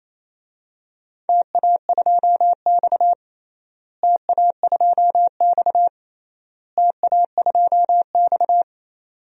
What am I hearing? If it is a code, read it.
Morse code, heard as a keyed tone, about 28 wpm, 700 Hz. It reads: TA2X TA2X TA2X
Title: DXCC-100 Call Signs - 4 Characters - Repeated 3x - 28wpm